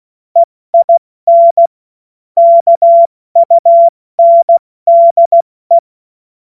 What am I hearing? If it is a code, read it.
Morse code, heard as a keyed tone, about 16 wpm, 680 Hz. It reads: EIN KUNDE
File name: Morsekurs / Witze / Witz 08. August 2023 70 F80